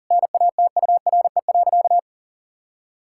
Morse code, heard as a keyed tone, about 40 wpm, 700 Hz. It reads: NATURE.
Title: Sentences from Top 1000 Words - Review Part 6 - 40wpm